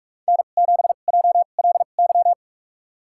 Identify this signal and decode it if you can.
Morse code, heard as a keyed tone, about 46 wpm, spaced as for 33 wpm, 700 Hz. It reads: N7JLY
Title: USA Call Signs - 5 Characters - 2x Character Spacing - 45wpm